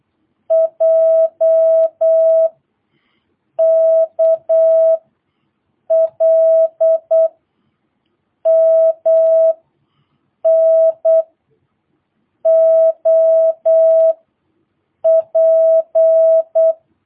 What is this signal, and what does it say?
Morse code, heard as a keyed tone, about 8 wpm, 650 Hz. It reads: J K L M N O P
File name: jh8cbh / 2022cw